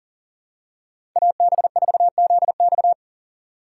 Morse code, heard as a keyed tone, about 40 wpm, 700 Hz. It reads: AB4ZX
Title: USA Call Signs - 5 Characters - 2x Character Spacing - 40wpm